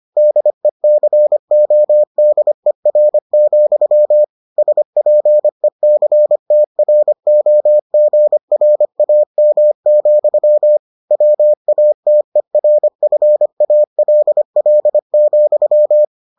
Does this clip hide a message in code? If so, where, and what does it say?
Yes: Morse code from 0.2 to 16.0 s, heard as a keyed tone, about 25 wpm, 600 Hz. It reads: DECODER, SPECTROGRAM, WATERFALL,